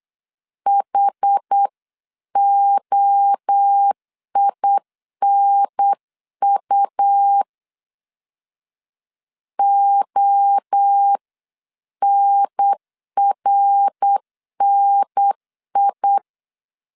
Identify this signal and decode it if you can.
Morse code, heard as a keyed tone, about 8 wpm, 800 Hz. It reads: HOINU ONRNI